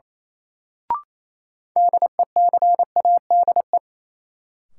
Morse code, heard as a keyed tone, about 28 wpm, 700 Hz. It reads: DECADE
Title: Sets of 5 Words - New Word Drill - 28wpm